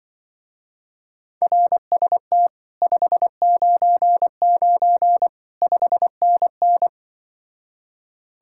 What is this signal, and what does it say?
Morse code, heard as a keyed tone, about 24 wpm, 700 Hz. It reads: RST 599 5NN
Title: Speed-Racing Common QSO Phrases Encore - 20wpm